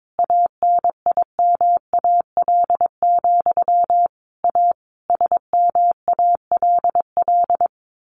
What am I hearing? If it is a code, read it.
Morse code, heard as a keyed tone, about 22 wpm, 700 Hz. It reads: ANIMAL, A SMALL
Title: A Sound Of Thunder 22wpm